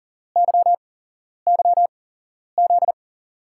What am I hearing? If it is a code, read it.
Morse code, heard as a keyed tone, about 40 wpm, spaced as for 11 wpm, 700 Hz. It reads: YYZ
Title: USA Call Signs - 6 Characters - 8x Character Spacing - 40wpm